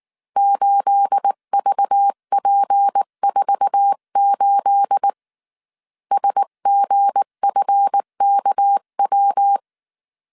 Morse code, heard as a keyed tone, about 19 wpm, 800 Hz. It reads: 8VP48 SGFKW
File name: CW / DET / 1 9 1500C S C 18wpm 2539